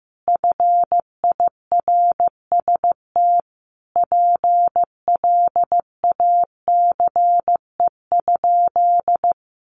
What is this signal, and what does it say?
Morse code, heard as a keyed tone, about 15 wpm, 700 Hz. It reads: FIRST PLACE?
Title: Sentences from Top 200 Words 15wpm